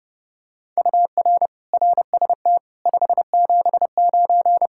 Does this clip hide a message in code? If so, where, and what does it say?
Yes: Morse code from 0.8 to 4.7 s, heard as a keyed tone, about 32 wpm, 700 Hz. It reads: UR RST 579